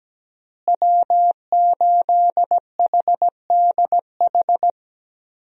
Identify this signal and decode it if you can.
Morse code, heard as a keyed tone, about 17 wpm, 700 Hz. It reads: W8HDH
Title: USA Call Signs - 5 Characters - Repeated 2x - 17wpm